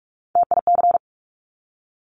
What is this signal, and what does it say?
Morse code, heard as a keyed tone, about 46 wpm, 700 Hz. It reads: TIC
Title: Top Three Letter Combinations in Words 45wpm